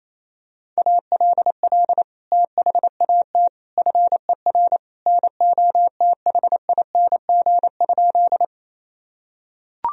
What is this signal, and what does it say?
Morse code, heard as a keyed tone, about 28 wpm, 700 Hz. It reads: ALL THAT FER NOTHING?
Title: Sentences from Top 500 Words and Abbreviations 28wpm